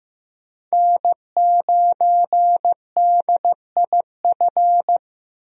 Morse code, heard as a keyed tone, about 15 wpm, 700 Hz. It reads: N9DIF
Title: USA Call Signs - 5 Characters - 15wpm